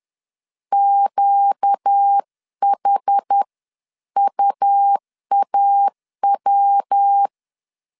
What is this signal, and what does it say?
Morse code, heard as a keyed tone, about 10 wpm, spaced as for 9 wpm, 800 Hz. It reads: QHUAW
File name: LEZIONI / 6 PKX0 / 6 PK 10wpm 2436